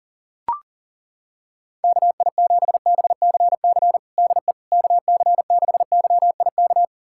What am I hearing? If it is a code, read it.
Morse code, heard as a keyed tone, about 37 wpm, 700 Hz. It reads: KI7BCC DE KC6YIK
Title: USA Callsign DE Callsign 40wpm